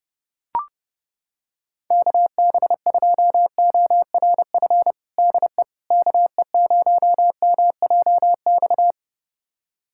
Morse code, heard as a keyed tone, about 30 wpm, 700 Hz. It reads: KB2ORF DE KE0MJX
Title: USA Callsign DE Callsign - 6 Characters - 30wpm